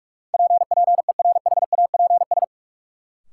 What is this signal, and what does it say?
Morse code, heard as a keyed tone, about 45 wpm, 700 Hz. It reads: PPERHAPS
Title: Sentences from Top 1100 Words - New Word Drill Alt - 45wpm